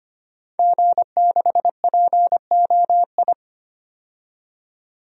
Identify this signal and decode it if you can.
Morse code, heard as a keyed tone, about 25 wpm, 700 Hz. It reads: G6POI
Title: DXCC-100 Call Signs - 5 Characters - 25wpm